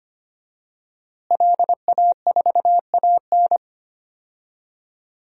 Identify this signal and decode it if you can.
Morse code, heard as a keyed tone, about 25 wpm, 700 Hz. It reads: LA4AN
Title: DXCC-100 Call Signs - 5 Characters - Repeated 2x - 25wpm